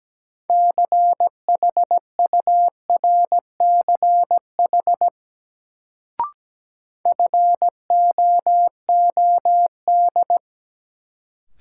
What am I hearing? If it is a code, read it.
Morse code, heard as a keyed tone, about 17 wpm, 700 Hz. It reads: CHURCH FOOD
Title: Sets of 5 Words - New Word Drill - 17wpm